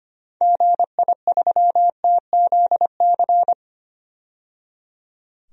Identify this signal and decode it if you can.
Morse code, heard as a keyed tone, about 25 wpm, 700 Hz. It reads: GI3TZC
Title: DXCC-100 Call Signs - 6 Characters - 25wpm